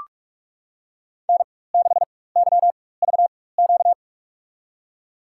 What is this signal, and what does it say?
Morse code, heard as a keyed tone, about 46 wpm, spaced as for 21 wpm, 700 Hz. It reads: N6YVQ
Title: USA Call Signs - 5 Characters - 4x Character Spacing - 45wpm